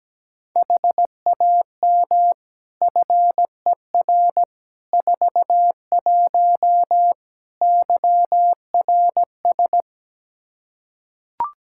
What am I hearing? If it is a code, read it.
Morse code, heard as a keyed tone, about 17 wpm, 700 Hz. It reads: HAM FER 41 YRS
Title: Practice QSO - Exchange 3 - More Info - 17wpm